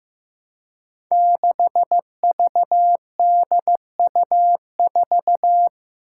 Morse code, heard as a keyed tone, about 15 wpm, 700 Hz. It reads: 6VDU4
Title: Five Challenging Letter-Number Combinations 15wpm